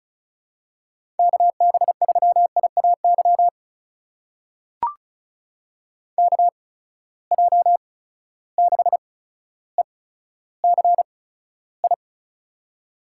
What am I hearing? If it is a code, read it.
Morse code, heard as a keyed tone, about 36 wpm, spaced as for 15 wpm, 700 Hz. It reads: KB3IAY KJ6ECI
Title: USA Call Signs - 6 Characters - 8x Character Spacing - 35wpm